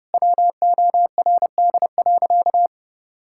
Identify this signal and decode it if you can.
Morse code, heard as a keyed tone, about 30 wpm, 700 Hz. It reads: WORD.
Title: Pangram Challenge v2 - 30wpm